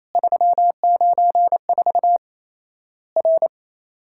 Morse code, heard as a keyed tone, about 28 wpm, 700 Hz. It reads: 394 R
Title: Worked All Europe DX Contest - QTC Exchange - 28wpm